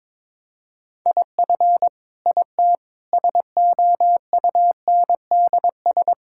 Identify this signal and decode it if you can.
Morse code, heard as a keyed tone, about 21 wpm, 700 Hz. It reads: IF IT SOUNDS
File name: Sentences with 9-Word Phrases - 22wpm